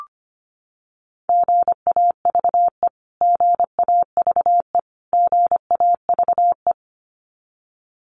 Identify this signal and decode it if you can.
Morse code, heard as a keyed tone, about 25 wpm, 700 Hz. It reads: GAVE GAVE GAVE
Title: Top 500 Words Repeated 25wpm